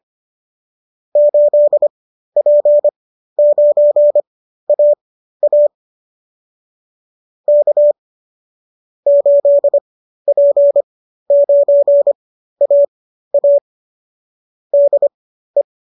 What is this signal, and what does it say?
Morse code, heard as a keyed tone, about 25 wpm, spaced as for 13 wpm, 570 Hz. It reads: 8P9AA K 8P9AA DE